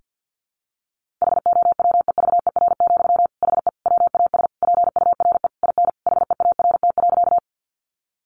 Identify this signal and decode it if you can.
Morse code, heard as a keyed tone, about 49 wpm, 700 Hz. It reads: HOWEVER, HE WAS PURE IN HEART.
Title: Sentences from Top 1500 Words 50wpm